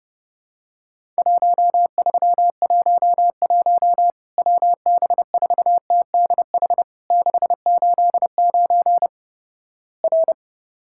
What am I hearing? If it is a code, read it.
Morse code, heard as a keyed tone, about 30 wpm, 700 Hz. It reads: 1311 WB4TDH 689 R
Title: Worked All Europe DX Contest - QTC Exchange - 30wpm